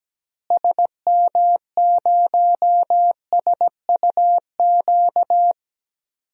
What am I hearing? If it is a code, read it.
Morse code, heard as a keyed tone, about 17 wpm, 700 Hz. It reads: SM0SUQ
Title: DXCC-100 Call Signs - 6 Characters - Repeated 3x - 17wpm